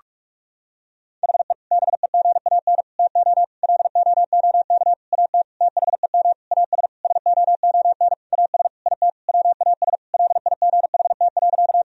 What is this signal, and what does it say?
Morse code, heard as a keyed tone, about 46 wpm, 700 Hz. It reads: HE BEGAN TO LOOK AT THEM AS SOON AS IT WAS LIGHT.
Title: Sentences from Top 1000 Words - Review Part 1 - 45wpm